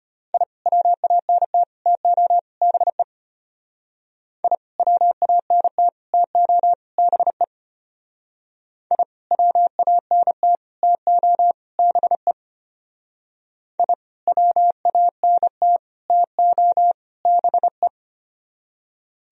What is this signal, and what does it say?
Morse code, heard as a keyed tone, about 30 wpm, 700 Hz. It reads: I WANT TO BE I WANT TO BE I WANT TO BE I WANT TO BE
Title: Speed-Racing - Sets of 4 Words 25wpm